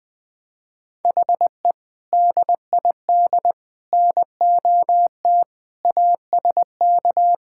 Morse code, heard as a keyed tone, about 20 wpm, 700 Hz. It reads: HE DID NOT ASK
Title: Sentences from Top 1600 Words 20wpm